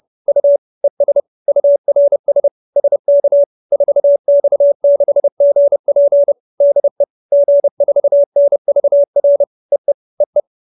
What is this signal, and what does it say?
Morse code, heard as a keyed tone, about 30 wpm, 570 Hz. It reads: U ES URS SK 4X6GP DE G4NVR EE EE
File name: QSO315 30